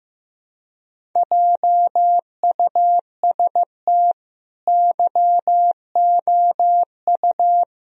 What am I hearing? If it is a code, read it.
Morse code, heard as a keyed tone, about 15 wpm, 700 Hz. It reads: JUST YOU
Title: Sentences from Top 1000 Words - Review Part 1 - 15wpm